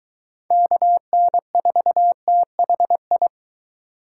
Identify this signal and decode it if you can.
Morse code, heard as a keyed tone, about 23 wpm, 700 Hz. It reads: KN4THI